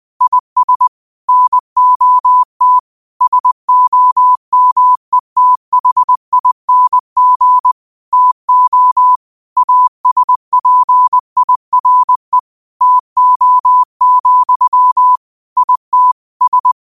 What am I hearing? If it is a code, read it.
Morse code, heard as a keyed tone, about 20 wpm, 1000 Hz. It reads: IS NOT SOMETHING TO ASPIRE TO, IT S
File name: qotdfiles / 20wpm